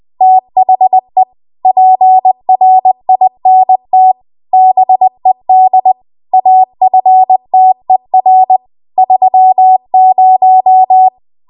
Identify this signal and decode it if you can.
Morse code, heard as a keyed tone, about 20 wpm, 750 Hz. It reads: THE PRINT BED AFTER 30
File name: Archive / 20 WPM / 20WPM